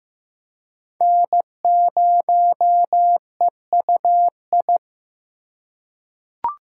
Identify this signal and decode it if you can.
Morse code, heard as a keyed tone, about 16 wpm, 700 Hz. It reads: N0EUI
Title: USA Call Signs - 5 Characters - 2x Character Spacing - 15wpm